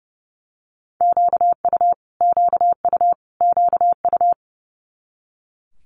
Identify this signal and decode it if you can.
Morse code, heard as a keyed tone, about 30 wpm, 700 Hz. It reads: QU QU QU
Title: Multi-letter Phonemes Repeated 3x 30wpm